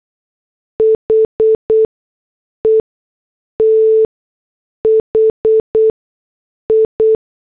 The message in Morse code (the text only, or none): HE THI